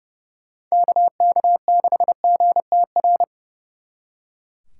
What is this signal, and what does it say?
Morse code, heard as a keyed tone, about 35 wpm, 700 Hz. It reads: KK6GTR
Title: Speed Racing - USA Call Signs 30wpm